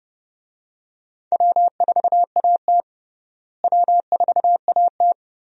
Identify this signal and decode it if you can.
Morse code, heard as a keyed tone, about 31 wpm, 700 Hz. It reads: W4AT W4AT
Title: USA Call Signs - 4 Characters - Repeated 2x - 30wpm